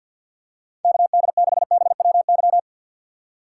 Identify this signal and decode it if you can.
Morse code, heard as a keyed tone, about 53 wpm, 700 Hz. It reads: KD6BWY
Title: USA Call Signs - 6x Character Spacing - 50wpm